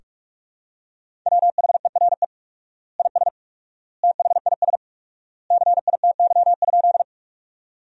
Morse code, heard as a keyed tone, about 44 wpm, 700 Hz. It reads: WHERE IS THIS CITY?
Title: 4x Word-Spacing - Sentences from Top 300 Words - 45wpm